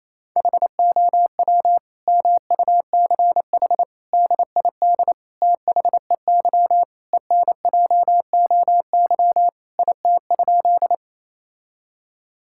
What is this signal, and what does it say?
Morse code, heard as a keyed tone, about 28 wpm, 700 Hz. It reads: HOW MUCH DID THEY ENJOY IT?
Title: Sentences from Top 1000 Words - Review Part 5 - 28wpm